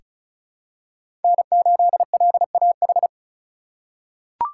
Morse code, heard as a keyed tone, about 34 wpm, 700 Hz. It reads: N8LAH
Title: USA Call Signs - 5 Characters - 3x Character Spacing - 35wpm